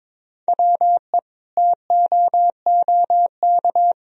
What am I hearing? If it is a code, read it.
Morse code, heard as a keyed tone, about 22 wpm, 700 Hz. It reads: WE TOOK
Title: Sentences from Top 400 Words and Abbreviations 22wpm